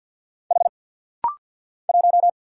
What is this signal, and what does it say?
Morse code, heard as a keyed tone, about 49 wpm, 700 Hz. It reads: H 1
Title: Single Letter-Number - Rapid-Fire 50wpm